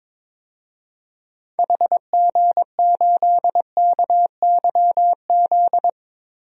Morse code, heard as a keyed tone, about 22 wpm, 700 Hz. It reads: HG8KYZ
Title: DXCC-100 Call Signs - 6 Characters - Repeated 2x - 22wpm